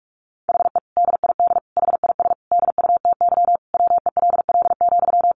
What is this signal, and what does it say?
Morse code, heard as a keyed tone, about 44 wpm, 700 Hz. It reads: HE DID HIS DUTY WELL,
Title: Sentences from Top 1500 Words 45wpm